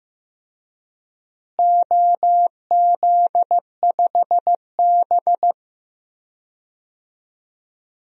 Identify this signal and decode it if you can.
Morse code, heard as a keyed tone, about 15 wpm, 700 Hz. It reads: OZ5B
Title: DXCC-100 Call Signs - 4 Characters - Repeated 3x - 15wpm